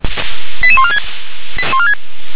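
Five-tone selective call (ZVEI), heard as quick runs of five tones, tones of about 70 ms. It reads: 88146
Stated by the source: ZVEI five-tone selective call received by SDR.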